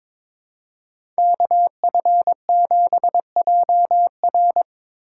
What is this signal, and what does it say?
Morse code, heard as a keyed tone, about 22 wpm, 700 Hz. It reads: KF7JR
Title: USA Call Signs - 5 Characters - 2x Character Spacing - 22wpm